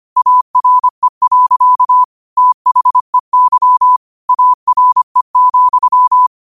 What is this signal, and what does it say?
Morse code, heard as a keyed tone, about 25 wpm, 1000 Hz. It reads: ARE. THEY ARE,
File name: qotdfiles / 25wpm